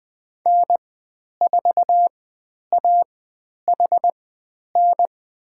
Morse code, heard as a keyed tone, about 20 wpm, spaced as for 10 wpm, 700 Hz. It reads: N4AHN
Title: 109 Lesson 18g - Letter m - Call Signs 20wpm